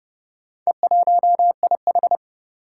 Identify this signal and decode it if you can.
Morse code, heard as a keyed tone, about 30 wpm, 700 Hz. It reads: E1IH
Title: Four Challenging Letter-Number Combinations 30wpm